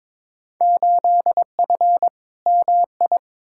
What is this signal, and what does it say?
Morse code, heard as a keyed tone, about 22 wpm, 700 Hz. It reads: 8F MI
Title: Field Day Exchanges Encore - 22wpm